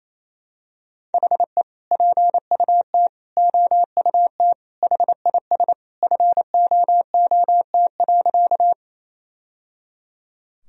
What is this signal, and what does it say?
Morse code, heard as a keyed tone, about 28 wpm, 700 Hz. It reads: HE PUT OUT HIS FOOT.
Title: Sentences from Top 1000 Words - Review Part 3 - 28wpm